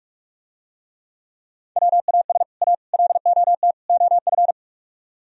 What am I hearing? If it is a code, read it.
Morse code, heard as a keyed tone, about 45 wpm, 700 Hz. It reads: WAS A LOT OF